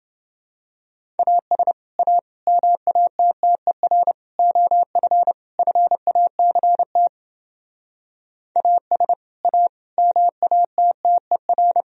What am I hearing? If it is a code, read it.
Morse code, heard as a keyed tone, about 29 wpm, 700 Hz. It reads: AS A MATTER OF FACT AS A MATTER